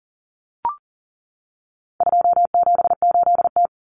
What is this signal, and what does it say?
Morse code, heard as a keyed tone, about 40 wpm, 700 Hz. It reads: 278T
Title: Common Cut Numbers - 4-Digit 40wpm